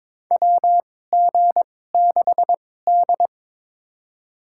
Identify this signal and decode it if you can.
Morse code, heard as a keyed tone, about 23 wpm, spaced as for 16 wpm, 700 Hz. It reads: WG6D
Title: USA Call Signs - 4 Characters - 2x Character Spacing - 22wpm